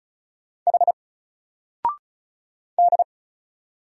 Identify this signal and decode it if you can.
Morse code, heard as a keyed tone, about 35 wpm, 700 Hz. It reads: H D
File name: Single-Letter-Number-35wpm